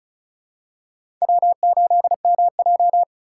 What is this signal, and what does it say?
Morse code, heard as a keyed tone, about 34 wpm, 700 Hz. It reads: W8MJ
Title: USA Call Signs - 3x Character Spacing - 35wpm